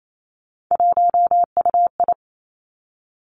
Morse code, heard as a keyed tone, about 28 wpm, 700 Hz. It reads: 1UI
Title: Three Challenging Letter-Number Combinations 28wpm